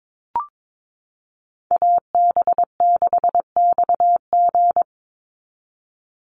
Morse code, heard as a keyed tone, about 22 wpm, 700 Hz. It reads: AB6XG
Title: Call Signs Repeated 2x - 22wpm